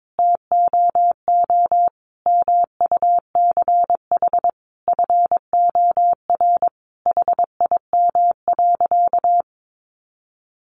Morse code, heard as a keyed tone, about 22 wpm, 700 Hz. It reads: TOO MUCH FOR HIM.
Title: Sentences from Top 1400 Words 22wpm